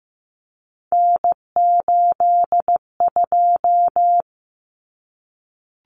Morse code, heard as a keyed tone, about 15 wpm, 700 Hz. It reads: N82
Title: Common Cut Numbers - 3-Digit 15wpm